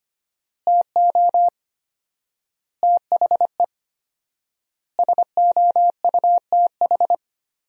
Morse code, heard as a keyed tone, about 25 wpm, 700 Hz. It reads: TO THE SOUTH
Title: 4x Word-Spacing - Sentences from Top 400 Words - 25wpm